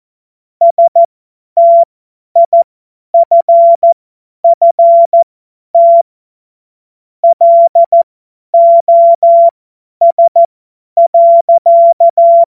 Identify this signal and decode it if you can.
Morse code, heard as a keyed tone, about 14 wpm, spaced as for 10 wpm, 680 Hz. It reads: STIFFT LOS.
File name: Morsekurs / Witze / Witz 02. Dezember 2025 50 F70